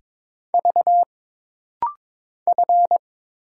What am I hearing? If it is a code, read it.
Morse code, heard as a keyed tone, about 22 wpm, 700 Hz. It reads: V F
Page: This rapid-fire timing is geared to push your ICR (Instant Character Recognition) skills.